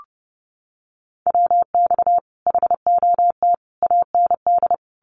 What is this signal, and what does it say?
Morse code, heard as a keyed tone, about 30 wpm, 700 Hz. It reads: WX HOT AND
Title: Common QSO Phrases Encore 30wpm